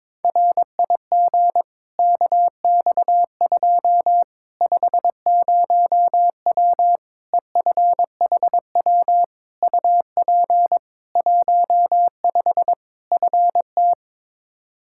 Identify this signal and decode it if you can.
Morse code, heard as a keyed tone, about 21 wpm, 700 Hz. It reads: RIG KX2 50W EFHW UP 15 FT
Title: K1USN SST QSO Practice 22wpm